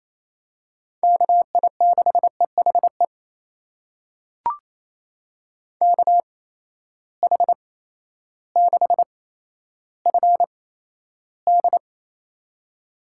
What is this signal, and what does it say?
Morse code, heard as a keyed tone, about 28 wpm, spaced as for 13 wpm, 700 Hz. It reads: KI6EHE KH6FD
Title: USA Call Signs - 8x Character Spacing - 28wpm